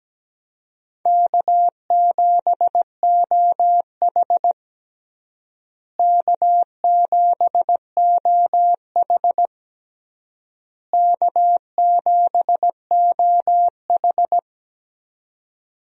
Morse code, heard as a keyed tone, about 17 wpm, 700 Hz. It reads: K7OH K7OH K7OH
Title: USA Call Signs - 4 Characters - Repeated 3x - 17wpm